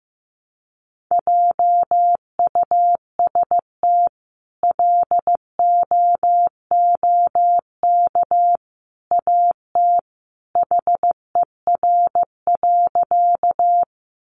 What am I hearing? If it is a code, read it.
Morse code, heard as a keyed tone, about 14 wpm, 700 Hz. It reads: JUST LOOK AT HER.
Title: Sentences from Top 200 Words 15wpm